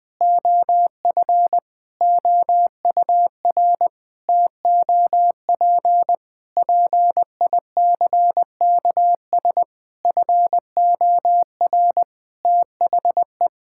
Morse code, heard as a keyed tone, about 20 wpm, 700 Hz. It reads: OF OUR TOP PICKS FOR THE
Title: Sentences with 9-Word Phrases - Spoken Beginning - 20wpm